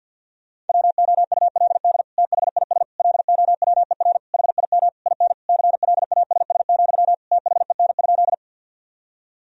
Morse code, heard as a keyed tone, about 50 wpm, 700 Hz. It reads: WOULD THIS LOWER HIM IN CLASS, THEN?
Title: Sentences from Top 1000 Words - Review Part 4 - 50wpm